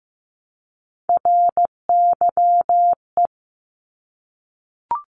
Morse code, heard as a keyed tone, about 15 wpm, 700 Hz. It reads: RYE
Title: Multi-letter Phonemes Repeated 3x 15wpm